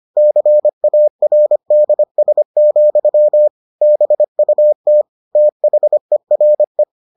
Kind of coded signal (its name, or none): Morse code